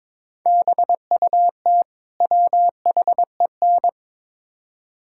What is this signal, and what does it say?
Morse code, heard as a keyed tone, about 22 wpm, 700 Hz. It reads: BUT WHEN
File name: practice / 22wpm / Build A Phrase - 5-Word Phrases - 22wpm